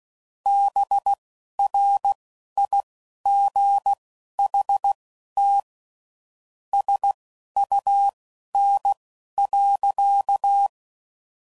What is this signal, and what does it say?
Morse code, heard as a keyed tone, about 16 wpm, spaced as for 11 wpm, 800 Hz. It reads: BRIGHT SUN.
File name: CW Code Groups @10WPM / Weather Sets / Weather 5 5